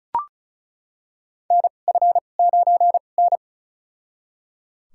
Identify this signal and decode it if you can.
Morse code, heard as a keyed tone, about 36 wpm, spaced as for 25 wpm, 700 Hz. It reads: NF9N
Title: USA Call Signs - 2x Character Spacing - 35wpm